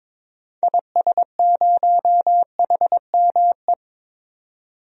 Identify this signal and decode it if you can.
Morse code, heard as a keyed tone, about 22 wpm, 700 Hz. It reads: IS0HME
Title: DXCC-100 Call Signs - 6 Characters - 22wpm